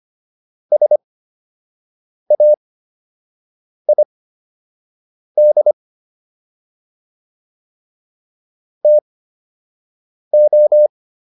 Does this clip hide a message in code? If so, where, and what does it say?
Yes: Morse code from 0.7 to 10.9 s, heard as a keyed tone, about 25 wpm, spaced as for 6 wpm, 600 Hz. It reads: SAID TO